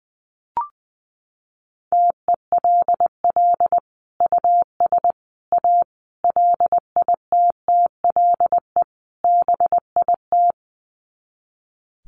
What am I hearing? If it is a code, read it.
Morse code, heard as a keyed tone, about 20 wpm, 700 Hz. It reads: TELL US A LITTLE BIT
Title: Sets of 5 Words Encore2 20wpm